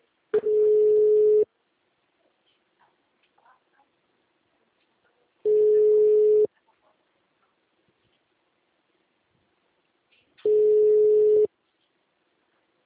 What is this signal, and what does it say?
Ringback tone (Europe, 425 Hz), with speech.